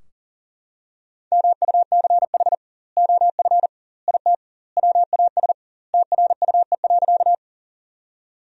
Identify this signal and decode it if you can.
Morse code, heard as a keyed tone, about 37 wpm, 700 Hz. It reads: MUCH OF IT WAS TRUE.
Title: 2x Word-Spacing - Sentences from Top 400 Words - 40wpm